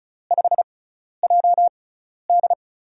Morse code, heard as a keyed tone, about 36 wpm, spaced as for 12 wpm, 700 Hz. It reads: 5JD
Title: USA Call Signs - 5 Characters - 6x Character Spacing - 35wpm